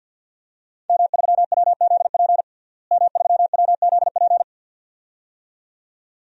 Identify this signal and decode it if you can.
Morse code, heard as a keyed tone, about 50 wpm, 700 Hz. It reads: M3WZP M3WZP
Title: DXCC-100 Call Signs - 5 Characters - Repeated 2x - 50wpm